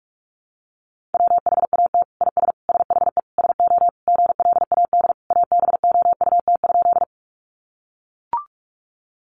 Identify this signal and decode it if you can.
Morse code, heard as a keyed tone, about 44 wpm, 700 Hz. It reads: WHAT IS SHE SO GLAD ABOUT?
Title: Sentences from Top 800 Words 45wpm